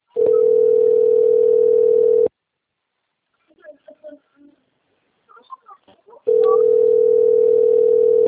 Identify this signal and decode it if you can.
Ringback tone (North America), 440 + 480 Hz, with speech.
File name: rg-1600-unknown- - - .61094